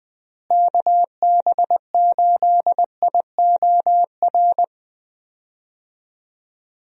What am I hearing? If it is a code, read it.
Morse code, heard as a keyed tone, about 20 wpm, 700 Hz. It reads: KB8IOR
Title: USA Call Signs - 6 Characters - Repeated 3x - 20wpm